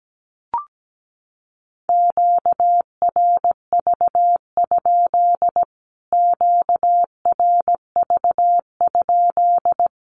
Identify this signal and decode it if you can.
Morse code, heard as a keyed tone, about 17 wpm, 700 Hz. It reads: QRV? QRV?
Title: Q-Codes Repeated 3x 17wpm